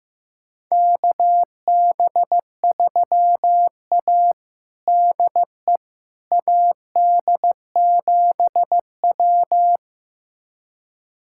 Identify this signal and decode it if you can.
Morse code, heard as a keyed tone, about 15 wpm, 700 Hz. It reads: KB3A DE AD7W
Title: USA Callsign DE Callsign - 4 Characters - 15wpm